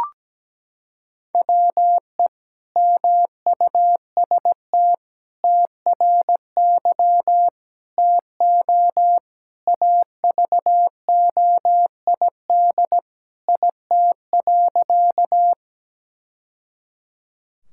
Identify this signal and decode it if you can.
Morse code, heard as a keyed tone, about 17 wpm, 700 Hz. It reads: WE MUST TRY TO AVOID IT.
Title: Sentences from Top 1000 Words - Review Part 6 - 17wpm